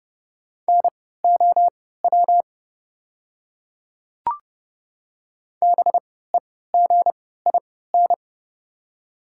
Morse code, heard as a keyed tone, about 30 wpm, spaced as for 17 wpm, 700 Hz. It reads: NOW BEGIN
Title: 3x Character Spacing - Top 300 Words - 30wpm